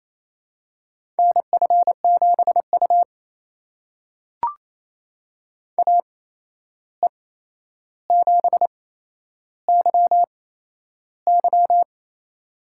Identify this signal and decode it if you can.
Morse code, heard as a keyed tone, about 28 wpm, spaced as for 11 wpm, 700 Hz. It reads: NF7U AE7YY
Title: USA Call Signs - 8x Character Spacing - 28wpm